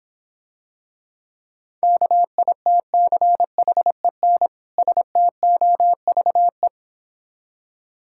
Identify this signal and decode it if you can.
Morse code, heard as a keyed tone, about 26 wpm, 700 Hz. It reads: KITCHEN STOVE